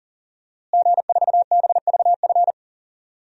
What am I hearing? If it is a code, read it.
Morse code, heard as a keyed tone, about 40 wpm, 700 Hz. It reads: G4BVF
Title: DXCC-100 Call Signs - 5 Characters - Repeated 3x - 40wpm